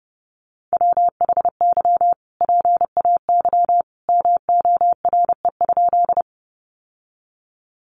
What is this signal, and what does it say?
Morse code, heard as a keyed tone, about 30 wpm, 700 Hz. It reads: WHY PAY MORE?
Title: Sentences from Top 700 Words 30wpm